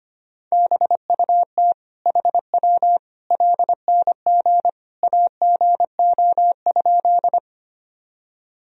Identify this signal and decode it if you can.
Morse code, heard as a keyed tone, about 25 wpm, 700 Hz. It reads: BUT HW LNG AGO?
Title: Sentences from Top 500 Words and Abbreviations 25wpm